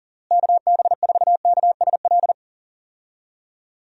Morse code, heard as a keyed tone, about 40 wpm, 700 Hz. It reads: KB4KSL
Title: USA Call Signs - 6 Characters - Repeated 3x - 40wpm